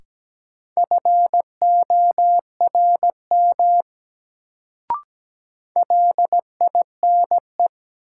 Morse code, heard as a keyed tone, about 17 wpm, 700 Hz. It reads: FORM LINE
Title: Sentences from Top 1000 Words - Review Part 1 - New Word Drill - 17wpm